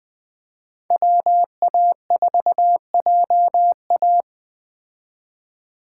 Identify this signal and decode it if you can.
Morse code, heard as a keyed tone, about 20 wpm, 700 Hz. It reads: WA4JA
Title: Speed Racing - USA Call Signs 17wpm